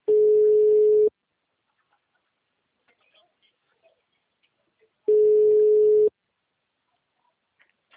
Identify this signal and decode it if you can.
Ringback tone (Europe, 425 Hz).